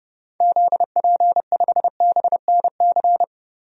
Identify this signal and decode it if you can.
Morse code, heard as a keyed tone, about 29 wpm, 700 Hz. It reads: ZP5BNC
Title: DXCC-100 Call Signs - 6 Characters - 30wpm